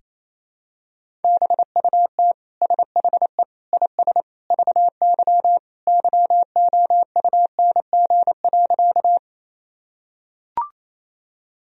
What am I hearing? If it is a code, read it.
Morse code, heard as a keyed tone, about 28 wpm, 700 Hz. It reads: BUT SHE IS VY YOUNG.
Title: Sentences from Top 400 Words and Abbreviations 28wpm